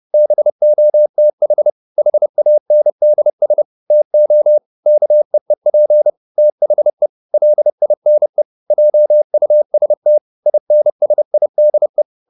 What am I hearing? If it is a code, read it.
Morse code, heard as a keyed tone, about 30 wpm, 600 Hz. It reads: BOTH HANDS TO KEEP THE LINE JUST INSIDE
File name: the old man and the sea / the old man and the sea 1 30wpm